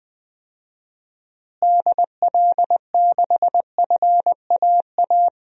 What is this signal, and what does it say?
Morse code, heard as a keyed tone, about 19 wpm, 700 Hz. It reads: DL6FAA
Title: DXCC-100 Call Signs - 6 Characters - Repeated 3x - 20wpm